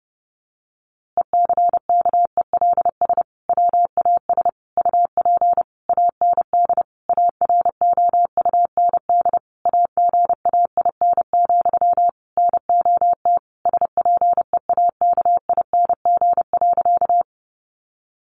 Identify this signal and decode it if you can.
Morse code, heard as a keyed tone, about 30 wpm, 700 Hz. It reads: ECKELS WAS UP AND AROUND AGAIN, NOT SPEAKING.
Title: A Sound Of Thunder 30wpm